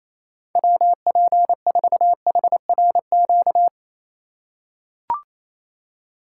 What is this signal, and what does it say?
Morse code, heard as a keyed tone, about 28 wpm, 700 Hz. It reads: WP4HRQ
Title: USA Call Signs - 6 Characters - 3x Character Spacing - 28wpm